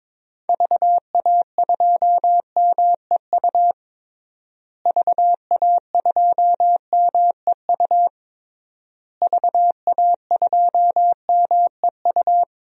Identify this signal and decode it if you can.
Morse code, heard as a keyed tone, about 22 wpm, 700 Hz. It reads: VA2MEU VA2MEU VA2MEU
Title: DXCC-100 Call Signs - 6 Characters - Repeated 3x - 22wpm